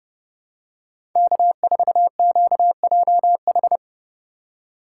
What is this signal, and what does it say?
Morse code, heard as a keyed tone, about 30 wpm, 700 Hz. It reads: K4QJH
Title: USA Call Signs - 5 Characters - 4x Character Spacing - 30wpm